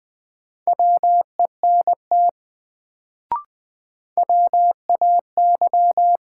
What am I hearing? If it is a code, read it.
Morse code, heard as a keyed tone, about 20 wpm, 700 Hz. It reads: WENT WAY
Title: Sets of 2 Words - New Word Drill - 20wpm